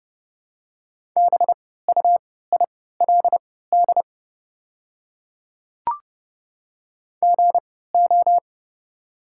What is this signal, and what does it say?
Morse code, heard as a keyed tone, about 31 wpm, spaced as for 17 wpm, 700 Hz. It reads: BUILD GO
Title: 3x Character Spacing - Top 200 Words - 30wpm